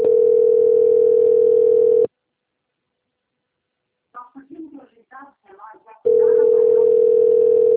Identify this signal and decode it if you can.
Ringback tone (North America), 440 + 480 Hz, with speech.
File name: rg-1600-unknown- - - .58677